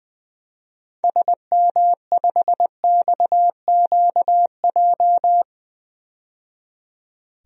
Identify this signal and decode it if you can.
Morse code, heard as a keyed tone, about 21 wpm, 700 Hz. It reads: SM5XQJ
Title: DXCC-100 Call Signs - 6 Characters - 20wpm